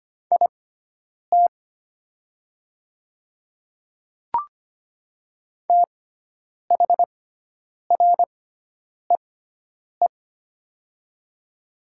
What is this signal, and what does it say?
Morse code, heard as a keyed tone, about 25 wpm, spaced as for 9 wpm, 700 Hz. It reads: IT THREE
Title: 6x Character Spacing - Top 300 Words - 25wpm